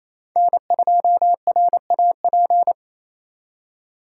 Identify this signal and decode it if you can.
Morse code, heard as a keyed tone, about 34 wpm, 700 Hz. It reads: N2RAP
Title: Speed Racing - USA Call Signs 28wpm